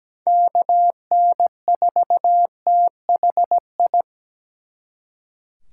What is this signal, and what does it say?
Morse code, heard as a keyed tone, about 20 wpm, 700 Hz. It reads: KN4THI
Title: Speed Racing - USA 2x3 Call Signs 17wpm